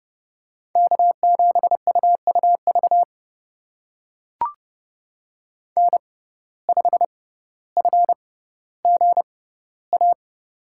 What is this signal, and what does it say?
Morse code, heard as a keyed tone, about 32 wpm, spaced as for 16 wpm, 700 Hz. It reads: K7UUV N5FGA
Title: USA Call Signs - 5 Characters - 6x Character Spacing - 30wpm